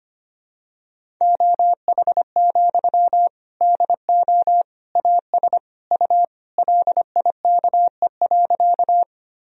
Morse code, heard as a keyed tone, about 25 wpm, 700 Hz. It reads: OH, DO AS U LIKE.
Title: Sentences from Top 500 Words and Abbreviations 25wpm